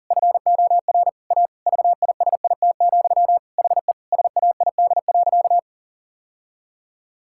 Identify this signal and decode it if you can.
Morse code, heard as a keyed tone, about 40 wpm, 700 Hz. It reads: FOR A VISIT, HE SAID.
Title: Sentences from Top 1000 Words - Review Part 6 - 40wpm